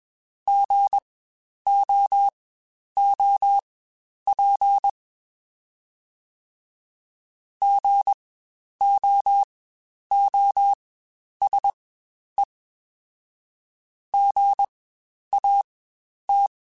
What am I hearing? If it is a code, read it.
Morse code, heard as a keyed tone, about 21 wpm, spaced as for 10 wpm, 800 Hz. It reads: GOOP GOOSE GAT
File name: Spaced Alphas / Spaced Alphas mp3s / Gs sp @20